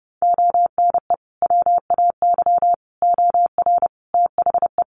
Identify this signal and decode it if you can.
Morse code, heard as a keyed tone, about 30 wpm, 700 Hz. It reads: ONE WAY OR THE
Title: Sets of 5 Words 30wpm